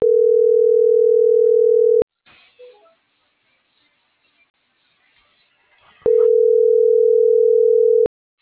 Ringback tone (North America), 440 + 480 Hz.